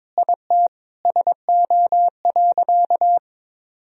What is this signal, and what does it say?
Morse code, heard as a keyed tone, about 21 wpm, 700 Hz. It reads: IT SO.
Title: Sentences from Top 1000 Words - Review Part 4 - 22wpm